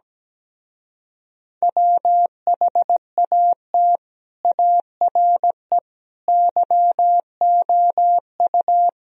Morse code, heard as a keyed tone, about 17 wpm, 700 Hz. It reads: WHAT ARE YOU
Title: Speed-Racing - Sets of 5 Words 17wpm